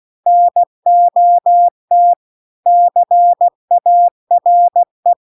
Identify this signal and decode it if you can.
Morse code, heard as a keyed tone, about 16 wpm, 700 Hz. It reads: NOT CARE
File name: CW-The-Big-Twoheaded-River-2-16WPM